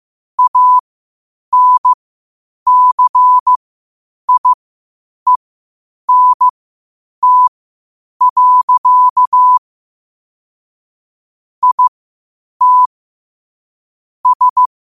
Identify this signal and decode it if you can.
Morse code, heard as a keyed tone, about 15 wpm, spaced as for 8 wpm, 1000 Hz. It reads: ANCIENT. IT S
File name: qotdfiles / 8wpm